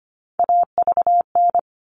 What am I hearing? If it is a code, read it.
Morse code, heard as a keyed tone, about 25 wpm, 700 Hz. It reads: AVN
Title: All Cut Numbers - 3-Digit 25wpm